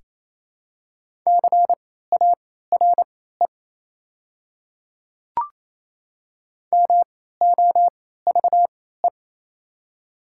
Morse code, heard as a keyed tone, about 28 wpm, spaced as for 16 wpm, 700 Hz. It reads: CARE MOVE
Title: 3x Character Spacing - Top 300 Words - 28wpm